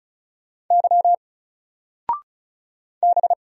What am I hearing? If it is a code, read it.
Morse code, heard as a keyed tone, about 36 wpm, 700 Hz. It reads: Y B
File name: Single-Letter-Number-35wpm